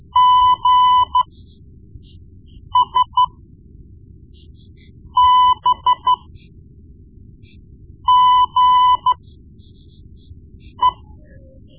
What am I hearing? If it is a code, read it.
Morse code, heard as a keyed tone, about 13 wpm, spaced as for 4 wpm, 1000 Hz. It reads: GSBGE
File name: Morse Lessons / Morse Lessons / Tape-1B